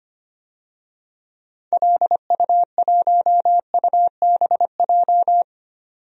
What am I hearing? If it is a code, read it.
Morse code, heard as a keyed tone, about 25 wpm, 700 Hz. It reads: LU1UBJ
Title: DXCC-100 Call Signs - 6 Characters - 25wpm